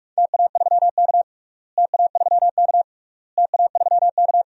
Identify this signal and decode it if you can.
Morse code, heard as a keyed tone, about 46 wpm, 700 Hz. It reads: TA3K TA3K TA3K
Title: DXCC-100 Call Signs - 4 Characters - Repeated 3x - 45wpm